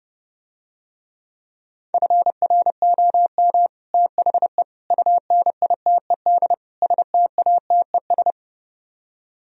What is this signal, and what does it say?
Morse code, heard as a keyed tone, about 30 wpm, 700 Hz. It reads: FROM THE UNITED STATES